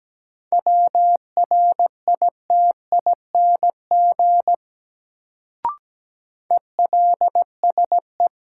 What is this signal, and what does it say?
Morse code, heard as a keyed tone, about 17 wpm, 700 Hz. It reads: WRITING ELSE
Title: Sentences from Top 800 Words - New Word Drill - 17wpm